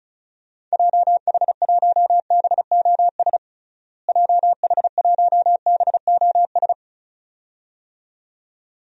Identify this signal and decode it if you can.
Morse code, heard as a keyed tone, about 36 wpm, 700 Hz. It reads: JH1BOS JH1BOS
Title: DXCC-100 Call Signs - 6 Characters - Repeated 2x - 35wpm